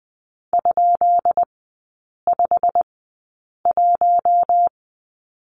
Morse code, heard as a keyed tone, about 20 wpm, spaced as for 8 wpm, 700 Hz. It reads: ?51
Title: 133 Lesson 22e - Question Mark - Three Characters 20wpm